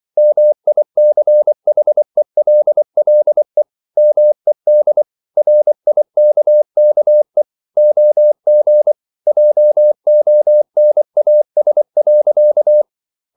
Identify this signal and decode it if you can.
Morse code, heard as a keyed tone, about 24 wpm, 600 Hz. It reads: MICHELLE MED RIKKE OG JONAS.
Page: Ferie 24 wpm